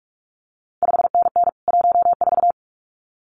Morse code, heard as a keyed tone, about 46 wpm, 700 Hz. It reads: 5NN 14
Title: CQ WW DX Contest Exchange - 45wpm